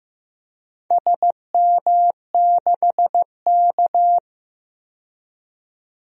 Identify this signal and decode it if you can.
Morse code, heard as a keyed tone, about 15 wpm, 700 Hz. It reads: SM6K
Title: DXCC-100 Call Signs - 4 Characters - 15wpm